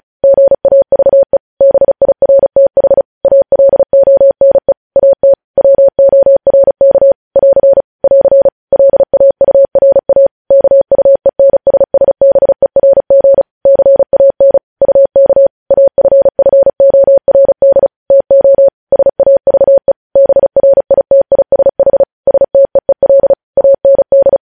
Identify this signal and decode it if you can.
Morse code, heard as a keyed tone, about 36 wpm, 550 Hz. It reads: GAVE BIRTH ALONE AT WORK <AR> <AR> LAURA KUENSSBERG CAN UK AFFORD TO SAVE BRITISH STEEL AND
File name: podcast / News headlines-1x-35x35